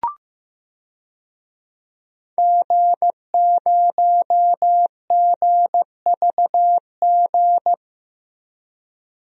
Morse code, heard as a keyed tone, about 15 wpm, 700 Hz. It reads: G0GVG
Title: DXCC-100 Call Signs - 5 Characters - 15wpm